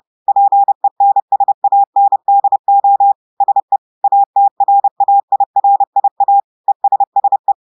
Morse code, heard as a keyed tone, about 30 wpm, 800 Hz. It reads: PENSANDO SE ATRAIRIA ESSE
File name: portugais texte en clair 2 vitesse 30 mots minutes